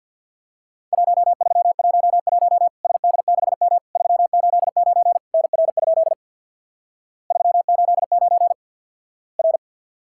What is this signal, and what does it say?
Morse code, heard as a keyed tone, about 50 wpm, 700 Hz. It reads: 1311 SD6M 389 NR? 389 R
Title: Worked All Europe DX Contest - QTC Exchange with Repeats - 50wpm